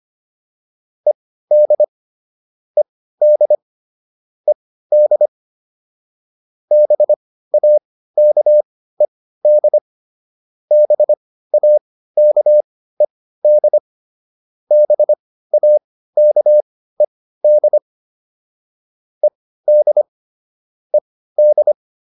Morse code, heard as a keyed tone, about 25 wpm, spaced as for 15 wpm, 600 Hz. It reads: ED ED ED BAKED BAKED BAKED ED ED